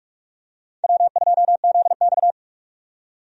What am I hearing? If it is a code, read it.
Morse code, heard as a keyed tone, about 45 wpm, 700 Hz. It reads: W2ZX
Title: USA Call Signs - 4 Characters - 45wpm